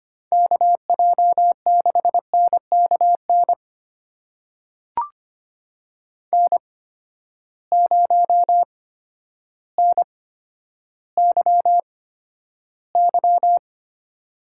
Morse code, heard as a keyed tone, about 25 wpm, spaced as for 11 wpm, 700 Hz. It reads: KJ6NKN N0NYY
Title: USA Call Signs - 8x Character Spacing - 25wpm